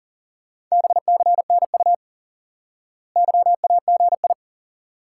Morse code, heard as a keyed tone, about 37 wpm, 700 Hz. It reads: BCNU YAGI
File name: IQE Course 3 - Lesson 89 - YAGI - Two Elements - 40wpm